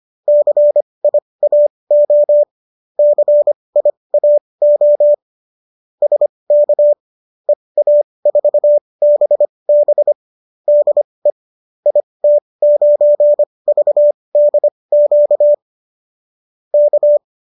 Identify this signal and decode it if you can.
Morse code, heard as a keyed tone, about 25 wpm, spaced as for 20 wpm, 590 Hz. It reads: CIAO CIAO SK EA4BB DE IT9VDQ K